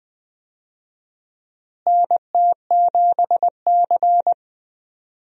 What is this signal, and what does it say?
Morse code, heard as a keyed tone, about 21 wpm, 700 Hz. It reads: NT7C
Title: USA Call Signs - 4 Characters - Repeated 3x - 20wpm